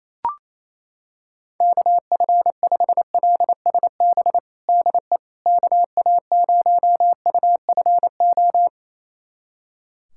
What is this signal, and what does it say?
Morse code, heard as a keyed tone, about 28 wpm, 700 Hz. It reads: KF5LSB DE KA0UFO
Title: USA Callsign DE Callsign 28wpm